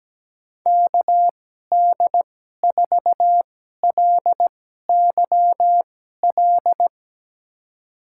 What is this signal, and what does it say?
Morse code, heard as a keyed tone, about 17 wpm, spaced as for 12 wpm, 700 Hz. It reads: KD4LYL
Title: USA Call Signs - 2x Character Spacing - 17wpm